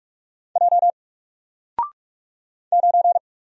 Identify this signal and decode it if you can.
Morse code, heard as a keyed tone, about 44 wpm, 700 Hz. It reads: J 9
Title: Single Character (Letter:Number:Punctuation:Prosign) Alt 45wpm